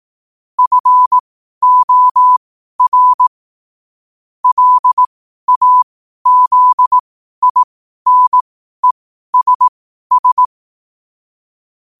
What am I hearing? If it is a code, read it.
Morse code, heard as a keyed tone, about 18 wpm, spaced as for 12 wpm, 1000 Hz. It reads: FOR LAZINESS
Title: 12 WPM morse code quotes for Thu, 29 Feb 2024